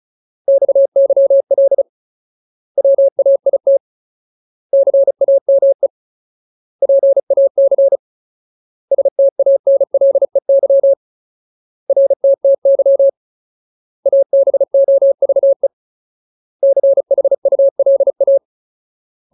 Morse code, heard as a keyed tone, about 34 wpm, 550 Hz. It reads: XYL WAIT CAME PAC STANLEY RTTY ABOVE CHULA
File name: Ultimate Word List / 35WPM Ultimate